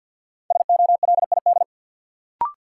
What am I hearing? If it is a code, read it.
Morse code, heard as a keyed tone, about 49 wpm, 700 Hz. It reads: SOLID
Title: Sentences from Top 1500 Words - New Word Drill - 50wpm